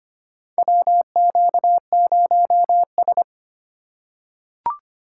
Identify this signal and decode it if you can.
Morse code, heard as a keyed tone, about 25 wpm, 700 Hz. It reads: WQ0S
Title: USA Call Signs - 4 Characters - 4x Character Spacing - 25wpm